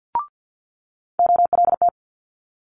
Morse code, heard as a keyed tone, about 50 wpm, 700 Hz. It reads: OLT
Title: Multi-letter Phonemes Quick Fire 50wpm